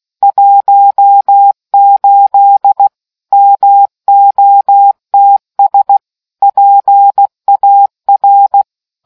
Morse code, heard as a keyed tone, about 16 wpm, 800 Hz. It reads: 18 MOTS PAR